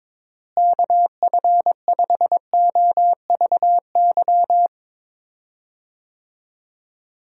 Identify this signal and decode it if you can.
Morse code, heard as a keyed tone, about 22 wpm, 700 Hz. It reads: KF5OVY
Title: USA Call Signs - 6 Characters - Repeated 2x - 22wpm